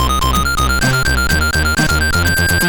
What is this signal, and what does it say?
There is no code.